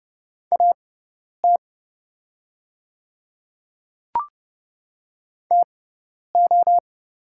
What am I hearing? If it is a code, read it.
Morse code, heard as a keyed tone, about 30 wpm, spaced as for 10 wpm, 700 Hz. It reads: AT TO
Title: 6x Character Spacing - Top 100 Words - 30wpm